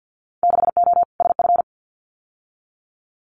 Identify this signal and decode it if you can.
Morse code, heard as a keyed tone, about 50 wpm, 700 Hz. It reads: 6O SF
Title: Winter Field Day Exchanges Encore 50wpm